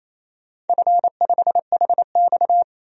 Morse code, heard as a keyed tone, about 29 wpm, 690 Hz. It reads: F5HX
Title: DXCC-100 Call Signs - 4 Characters - 28wpm